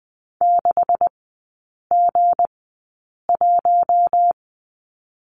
Morse code, heard as a keyed tone, about 20 wpm, spaced as for 8 wpm, 690 Hz. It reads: 6G1
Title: 191 Lesson 31e - Letter v - Three Characters 20wpm